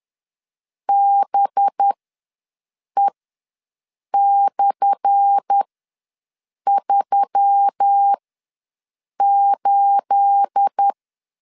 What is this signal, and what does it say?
Morse code, heard as a keyed tone, about 10 wpm, spaced as for 6 wpm, 800 Hz. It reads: BE/38
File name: LEZIONI / 10 SIGNS / 10C DOT ws 10wpm 4213